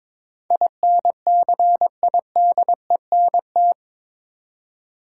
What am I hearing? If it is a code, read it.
Morse code, heard as a keyed tone, about 22 wpm, 700 Hz. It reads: INCIDENT
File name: practice / 22wpm / Eight Letter Words - 22wpm